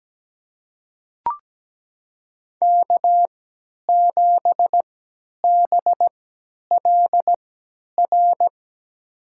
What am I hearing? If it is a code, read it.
Morse code, heard as a keyed tone, about 17 wpm, spaced as for 10 wpm, 700 Hz. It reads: K7BLR
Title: USA Call Signs - 5 Characters - 3x Character Spacing - 17wpm